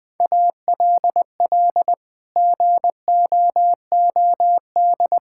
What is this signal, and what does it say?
Morse code, heard as a keyed tone, about 21 wpm, 700 Hz. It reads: ALL GOOD
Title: Sets of 8 Words - 20wpm